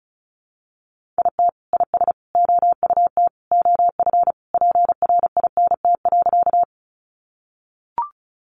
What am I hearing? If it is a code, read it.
Morse code, heard as a keyed tone, about 34 wpm, 700 Hz. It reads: IT IS OUT OF PRINT.
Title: Sentences from Top 1400 Words 35wpm